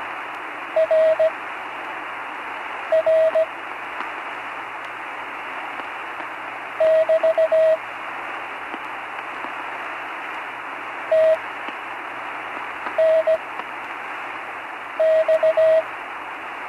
Morse code, heard as a keyed tone, about 17 wpm, spaced as for 5 wpm, 640 Hz, with radio noise. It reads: RR <BT> TNX
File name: CW Learning Programs / Learning Morse Code (from Tune in the World) / 35 answers